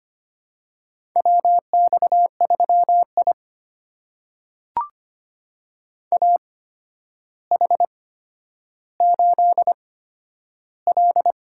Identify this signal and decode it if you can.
Morse code, heard as a keyed tone, about 25 wpm, spaced as for 11 wpm, 700 Hz. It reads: WX3I AH8L
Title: USA Call Signs - 8x Character Spacing - 25wpm